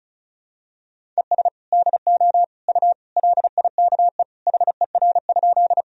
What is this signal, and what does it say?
Morse code, heard as a keyed tone, about 34 wpm, 700 Hz. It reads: ES DO U LIKE HER?
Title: Sentences from Top 100 Words and Abbreviations 35wpm